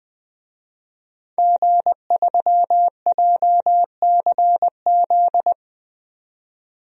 Morse code, heard as a keyed tone, about 20 wpm, 700 Hz. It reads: G3JCZ
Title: DXCC-100 Call Signs - 5 Characters - 20wpm